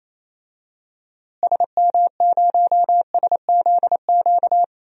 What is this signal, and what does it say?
Morse code, heard as a keyed tone, about 28 wpm, 700 Hz. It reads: SM0SZQ
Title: DXCC-100 Call Signs - 6 Characters - Repeated 3x - 28wpm